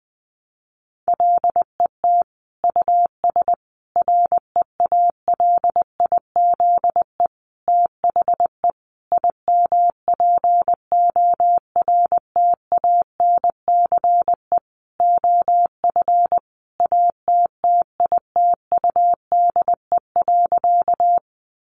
Morse code, heard as a keyed tone, about 20 wpm, 700 Hz. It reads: LET US REALIZE THE IMPORTANCE OF ATTITUDE.
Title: Sentences from Top 1300 Words 20wpm